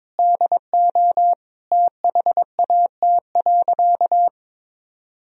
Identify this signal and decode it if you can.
Morse code, heard as a keyed tone, about 22 wpm, 700 Hz. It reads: DO THAT.
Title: Sentences from Top 1000 Words - Review Part 4 - 22wpm